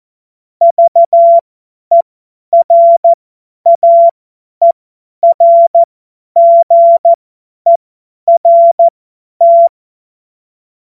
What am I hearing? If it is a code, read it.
Morse code, heard as a keyed tone, about 14 wpm, spaced as for 10 wpm, 680 Hz. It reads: VERAERGERT
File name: Morsekurs / Witze / Witz 18. Juli 2023 50 F70